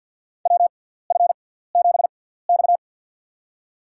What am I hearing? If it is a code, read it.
Morse code, heard as a keyed tone, about 52 wpm, spaced as for 17 wpm, 700 Hz. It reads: WF7X
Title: USA Call Signs - 4 Characters - 6x Character Spacing - 50wpm